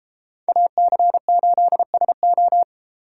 Morse code, heard as a keyed tone, about 32 wpm, 700 Hz. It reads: AC8SO